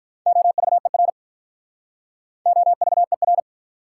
Morse code, heard as a keyed tone, about 43 wpm, 700 Hz. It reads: OVER OVER
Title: Speed-Racing Prefixes and Suffixes 35wpm